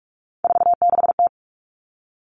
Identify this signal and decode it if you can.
Morse code, heard as a keyed tone, about 44 wpm, 700 Hz. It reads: 46T
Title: All Cut Numbers - 3-Digit 45wpm